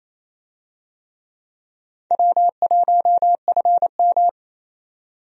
Morse code, heard as a keyed tone, about 28 wpm, 700 Hz. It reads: W1FM
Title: USA Call Signs - 4 Characters - 28wpm